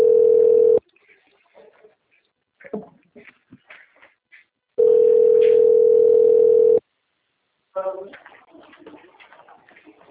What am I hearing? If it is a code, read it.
Ringback tone (North America), 440 + 480 Hz, with speech.
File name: out- - - - - .90699